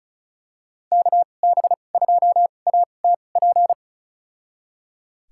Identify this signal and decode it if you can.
Morse code, heard as a keyed tone, about 36 wpm, spaced as for 25 wpm, 700 Hz. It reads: KB2ATP
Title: USA Call Signs - 6 Characters - 2x Character Spacing - 35wpm